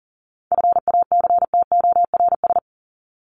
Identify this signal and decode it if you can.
Morse code, heard as a keyed tone, about 40 wpm, 700 Hz. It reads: FACTORS
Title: ES:S Suffix - Top 100 Words with Base Words 40wpm